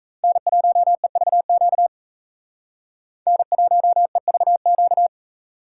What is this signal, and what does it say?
Morse code, heard as a keyed tone, about 36 wpm, 700 Hz. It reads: N1EVQ N1EVQ